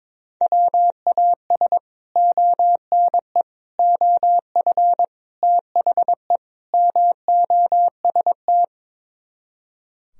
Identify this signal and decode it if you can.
Morse code, heard as a keyed tone, about 22 wpm, 700 Hz. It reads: WAS ONE OF THE MOST
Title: Speed-Racing - Sets of 5 Words 22wpm